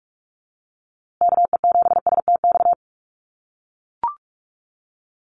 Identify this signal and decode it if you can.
Morse code, heard as a keyed tone, about 44 wpm, 700 Hz. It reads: KE7STX
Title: USA Call Signs Repeated 2x 45wpm